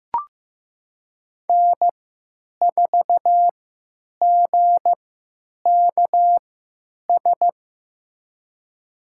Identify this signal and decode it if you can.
Morse code, heard as a keyed tone, about 15 wpm, spaced as for 8 wpm, 700 Hz. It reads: N4GKS
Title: USA Call Signs - 5 Characters - 3x Character Spacing - 15wpm